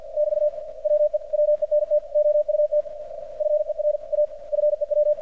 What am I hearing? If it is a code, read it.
Morse code, heard as a keyed tone, about 53 wpm, 600 Hz, with radio noise. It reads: <BT> WE WENT OUT FER A FEW
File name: archived / CRSnet